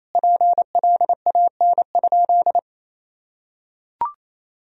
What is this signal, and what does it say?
Morse code, heard as a keyed tone, about 28 wpm, 700 Hz. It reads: PLAN?
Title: Sentences from Top 1000 Words - Review Part 4 - 28wpm